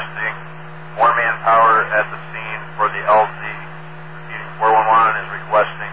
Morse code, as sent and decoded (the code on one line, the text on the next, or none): none